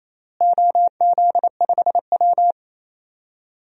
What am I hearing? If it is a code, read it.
Morse code, heard as a keyed tone, about 28 wpm, 700 Hz. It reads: OZ5W
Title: DXCC-100 Call Signs - 4 Characters - Repeated 3x - 28wpm